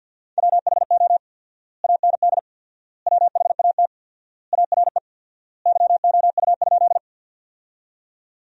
Morse code, heard as a keyed tone, about 50 wpm, 700 Hz. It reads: WHO AND WHAT ARE YOU?
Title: 4x Word-Spacing - Sentences from Top 100 Words - 50wpm